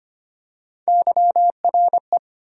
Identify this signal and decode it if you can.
Morse code, heard as a keyed tone, about 25 wpm, 700 Hz. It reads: YRE
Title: USA Call Sign Postfixes - 3 Letters 25wpm